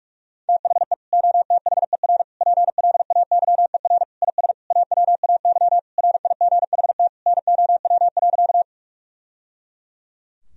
Morse code, heard as a keyed tone, about 46 wpm, 700 Hz. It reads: THE OTHER PLAYER IS AWAY RIGHT NOW.
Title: Sentences from Top 1000 Words - Review Part 4 - 45wpm